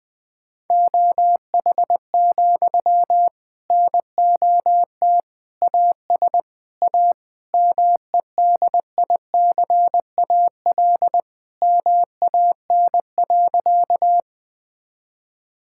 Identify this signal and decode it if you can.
Morse code, heard as a keyed tone, about 20 wpm, 700 Hz. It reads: OH, NOT AS A MEDICAL MAN.
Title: Sentences from Top 1000 Words - Review Part 5 - 20wpm